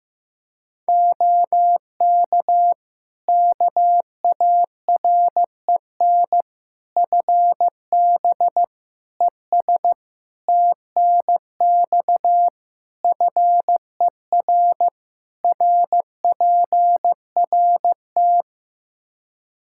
Morse code, heard as a keyed tone, about 15 wpm, 700 Hz. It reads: OK KAREN FB ES TNX FER RPRT
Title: Practice QSO - Exchange 2 - Info - 15wpm